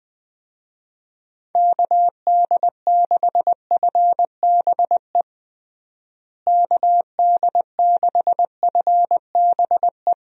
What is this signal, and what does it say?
Morse code, heard as a keyed tone, about 20 wpm, 700 Hz. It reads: KD6FBE KD6FBE
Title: USA Call Signs - 6 Characters - Repeated 2x - 20wpm